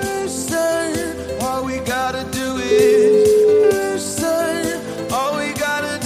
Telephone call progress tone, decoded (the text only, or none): ringback tone